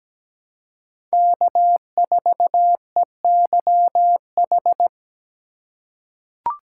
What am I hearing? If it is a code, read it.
Morse code, heard as a keyed tone, about 17 wpm, 700 Hz. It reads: K4EYH
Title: USA Call Signs - 2x Character Spacing - 17wpm